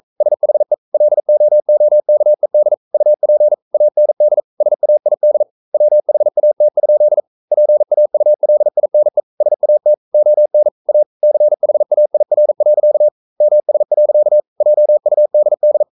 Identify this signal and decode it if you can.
Morse code, heard as a keyed tone, about 42 wpm, 600 Hz. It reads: SHE LOOKED UP AND SAID WHAT? PAULINE SAT ON A CHAIR. MS. JUDD
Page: Romanen Never af Ken Follet på engelsk.